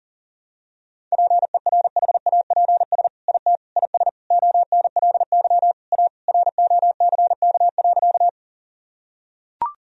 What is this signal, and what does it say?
Morse code, heard as a keyed tone, about 40 wpm, 700 Hz. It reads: PERHAPS IT IS ONLY A ROCK.
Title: Sentences from Top 1000 Words - Review Part 2 - 40wpm